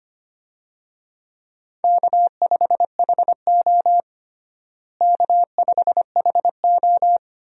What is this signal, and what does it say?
Morse code, heard as a keyed tone, about 25 wpm, 700 Hz. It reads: K5HO K5HO
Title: USA Call Signs - 4 Characters - Repeated 3x - 25wpm